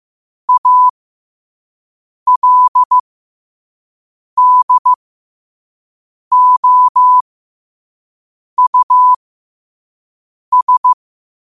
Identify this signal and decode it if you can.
Morse code, heard as a keyed tone, about 15 wpm, spaced as for 5 wpm, 1000 Hz. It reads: ALDOUS